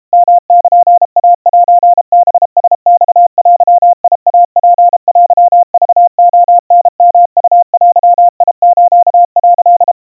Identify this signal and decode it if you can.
Morse code (Japanese Wabun code), heard as a keyed tone, about 32 wpm, 700 Hz. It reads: ヨルイセハラマデイツテクレタヨウデス」
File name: kawa / Morse sample / kamyu 120 cpm 700 Hz Crispy